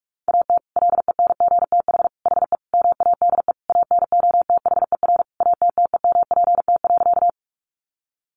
Morse code, heard as a keyed tone, about 46 wpm, 700 Hz. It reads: AT LENGTH HE MADE ANOTHER ATTEMPT.
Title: Sentences from Top 1000 Words 45wpm